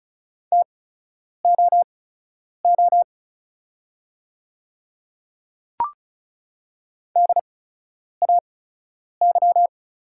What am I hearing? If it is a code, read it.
Morse code, heard as a keyed tone, about 36 wpm, spaced as for 10 wpm, 690 Hz. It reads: TOO DAY
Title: 8x Character Spacing - Top 200 Words - 35wpm